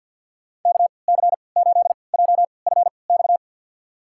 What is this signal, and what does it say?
Morse code, heard as a keyed tone, about 52 wpm, spaced as for 29 wpm, 700 Hz. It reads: KC8JFX
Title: USA Call Signs - 6 Characters - 3x Character Spacing - 50wpm